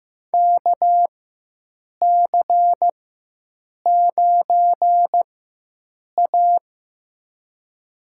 Morse code, heard as a keyed tone, about 15 wpm, spaced as for 7 wpm, 700 Hz. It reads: KC9A
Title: USA Call Signs - 4 Characters - 4x Character Spacing - 15wpm